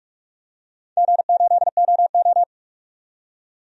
Morse code, heard as a keyed tone, about 41 wpm, 700 Hz. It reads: G8OO
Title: DXCC-100 Call Signs - 4 Characters - 45wpm